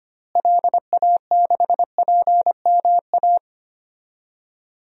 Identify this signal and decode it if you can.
Morse code, heard as a keyed tone, about 25 wpm, 700 Hz. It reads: LA6PMA
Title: DXCC-100 Call Signs - 6 Characters - Repeated 2x - 25wpm